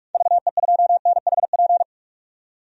Morse code, heard as a keyed tone, about 46 wpm, 690 Hz. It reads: VE2NHP
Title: DXCC-100 Call Signs - 6 Characters - 45wpm